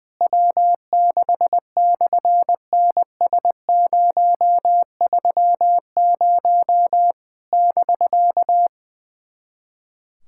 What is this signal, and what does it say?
Morse code, heard as a keyed tone, about 20 wpm, 700 Hz. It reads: W6/NS030 <BK>
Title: SOTA QSO Practice 20wpm